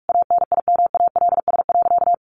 Morse code, heard as a keyed tone, about 44 wpm, 700 Hz. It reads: ANIMALS.
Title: A Sound Of Thunder 45wpm